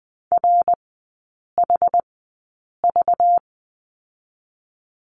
Morse code, heard as a keyed tone, about 20 wpm, spaced as for 8 wpm, 690 Hz. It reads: RHV
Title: 212 Lesson 34e - Number 8 - Three Characters 20wpm